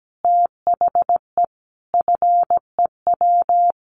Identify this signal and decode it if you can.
Morse code, heard as a keyed tone, about 17 wpm, 700 Hz. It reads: THE FEW
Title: Sets of 5 Words Encore 17wpm